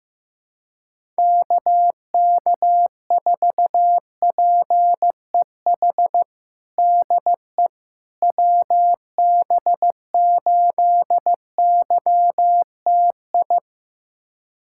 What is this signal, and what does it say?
Morse code, heard as a keyed tone, about 15 wpm, 700 Hz. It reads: KK4PEH DE WB8YTI
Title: USA Callsign DE Callsign - 6 Characters - 15wpm